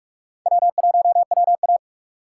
Morse code, heard as a keyed tone, about 44 wpm, 700 Hz. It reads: W1WA
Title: USA Call Signs - 4 Characters - 4x Character Spacing - 45wpm